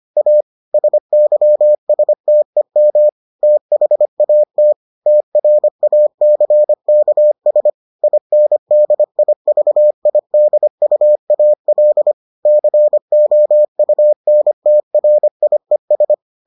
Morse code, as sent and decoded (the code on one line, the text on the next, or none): .- / ... -.-- ... - . -- / - .... .- - / - .-. .- -.-. -.- ... / .. -. -.. .. ...- .. -.. ..- .- .-.. / -.-. --- ..- -. - .-. .. ...
A SYSTEM THAT TRACKS INDIVIDUAL COUNTRIES